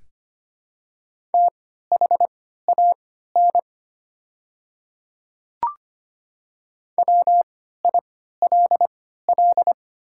Morse code, heard as a keyed tone, about 25 wpm, spaced as for 14 wpm, 700 Hz. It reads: THAN WILL
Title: 3x Character Spacing - Top 100 Words - 25wpm